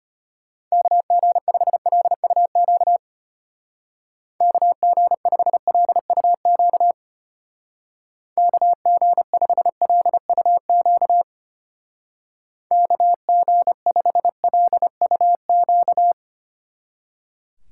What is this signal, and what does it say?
Morse code, heard as a keyed tone, about 31 wpm, 700 Hz. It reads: KG5LUQ KG5LUQ KG5LUQ KG5LUQ
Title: Speed Racing - USA 2x3 Call Signs 25wpm